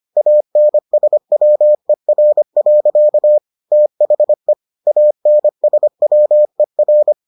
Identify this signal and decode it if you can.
Morse code, heard as a keyed tone, about 25 wpm, 600 Hz. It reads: ANSWER. THE ANSWER